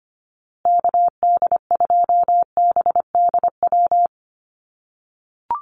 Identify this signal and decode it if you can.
Morse code, heard as a keyed tone, about 25 wpm, 700 Hz. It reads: KD2BDW
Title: USA Call Signs Repeated 3x 25wpm